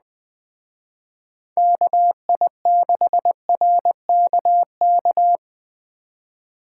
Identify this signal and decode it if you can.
Morse code, heard as a keyed tone, about 20 wpm, 700 Hz. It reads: KI6RKK
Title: Speed Racing - USA 2x3 Call Signs 17wpm